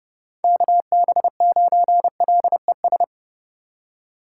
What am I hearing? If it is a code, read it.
Morse code, heard as a keyed tone, about 30 wpm, 700 Hz. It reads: KB9LES
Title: USA Call Signs - 6 Characters - 4x Character Spacing - 30wpm